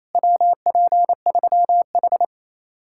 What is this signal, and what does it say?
Morse code, heard as a keyed tone, about 34 wpm, 700 Hz. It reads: WP3H
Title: Speed-Racing Common QSO Phrases - 28wpm